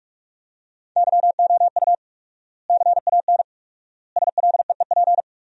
Morse code, heard as a keyed tone, about 46 wpm, 700 Hz. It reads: YOU CAN SLEEP
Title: 4x Word-Spacing - Sentences from Top 500 Words - 45wpm